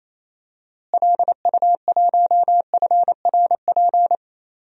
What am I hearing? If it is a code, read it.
Morse code, heard as a keyed tone, about 28 wpm, 700 Hz. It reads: LU1FRP
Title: DXCC-100 Call Signs - 6 Characters - Repeated 2x - 28wpm